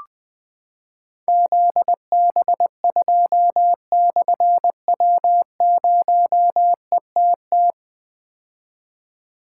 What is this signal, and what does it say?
Morse code, heard as a keyed tone, about 20 wpm, 700 Hz. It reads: ZB2/W0ETT
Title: Difficult Call Signs 20wpm